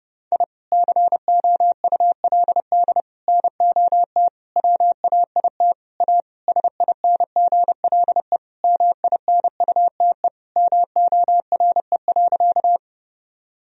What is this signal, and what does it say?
Morse code, heard as a keyed tone, about 32 wpm, 700 Hz. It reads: I COULD NOT WAIT A SINGLE MINUTE MORE.
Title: Sentences from Top 1000 Words - Review Part 4 - 30wpm